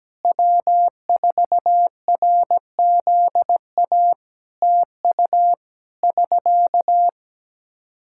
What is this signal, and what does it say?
Morse code, heard as a keyed tone, about 17 wpm, 700 Hz. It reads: W4RZA TU <SK>
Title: Practice QSO - Exchange 4 - Ending - 17wpm